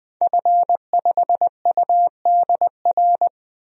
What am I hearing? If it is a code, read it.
Morse code, heard as a keyed tone, about 20 wpm, 700 Hz. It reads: F5UDR
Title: DXCC-100 Call Signs - 5 Characters - 20wpm